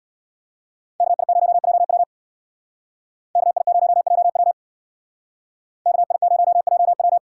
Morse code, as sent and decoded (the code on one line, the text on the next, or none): -.- .. ----- .--- .-- / -.- .. ----- .--- .-- / -.- .. ----- .--- .--
KI0JW KI0JW KI0JW